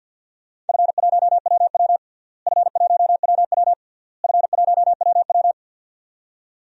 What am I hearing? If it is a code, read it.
Morse code, heard as a keyed tone, about 52 wpm, 700 Hz. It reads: F1WW F1WW F1WW
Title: DXCC-100 Call Signs - 4 Characters - Repeated 3x - 50wpm